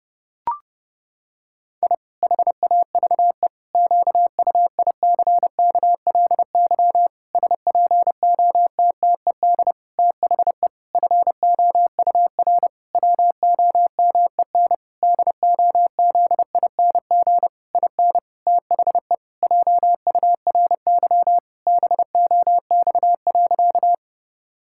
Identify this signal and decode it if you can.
Morse code, heard as a keyed tone, about 32 wpm, 700 Hz. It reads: I HAVE QUICKLY SPOTTED THE FOUR WOMEN DOZING IN THE JURY BOX.
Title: Pangram Challenge v2 - 30wpm